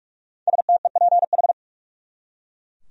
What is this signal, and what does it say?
Morse code, heard as a keyed tone, about 41 wpm, 700 Hz. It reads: STEPH
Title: USA Common Names - 45wpm